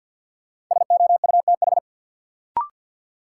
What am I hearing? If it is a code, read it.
Morse code, heard as a keyed tone, about 53 wpm, 700 Hz. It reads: SOUTH
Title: Sentences from Top 400 Words - New Word Drill - 50wpm